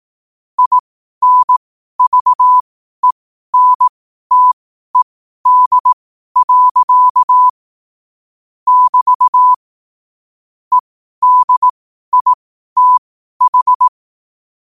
Morse code, heard as a keyed tone, about 18 wpm, spaced as for 12 wpm, 1000 Hz. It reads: INVENTED. <BT> EDITH
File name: qotdfiles / 12wpm